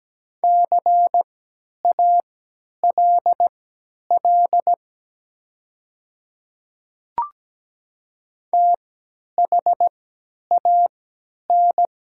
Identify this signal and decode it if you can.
Morse code, heard as a keyed tone, about 17 wpm, spaced as for 10 wpm, 700 Hz. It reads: CALL THAN
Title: 3x Character Spacing - Top 100 Words - 17wpm